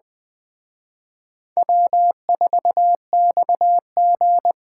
Morse code, heard as a keyed tone, about 20 wpm, 700 Hz. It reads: W4XG